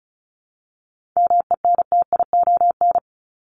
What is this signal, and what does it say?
Morse code, heard as a keyed tone, about 34 wpm, 700 Hz. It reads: MENTION
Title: Top 1501-1750 Words 35wpm